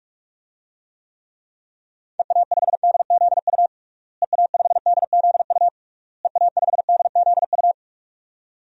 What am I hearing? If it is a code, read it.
Morse code, heard as a keyed tone, about 46 wpm, 700 Hz. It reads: EA5DZU EA5DZU EA5DZU
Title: DXCC-100 Call Signs - 6 Characters - Repeated 3x - 45wpm